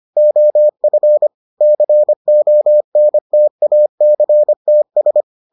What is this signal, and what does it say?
Morse code, heard as a keyed tone, about 25 wpm, 600 Hz. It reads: OF CONTACTS